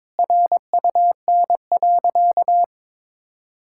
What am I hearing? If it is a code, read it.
Morse code, heard as a keyed tone, about 22 wpm, 700 Hz. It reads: RUN.
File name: Difficult Sentences with Hints - 22wpm